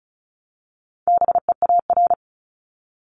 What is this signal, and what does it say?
Morse code, heard as a keyed tone, about 34 wpm, 700 Hz. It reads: BEAR
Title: ING Suffix - Top 100 Words with Base Words 35wpm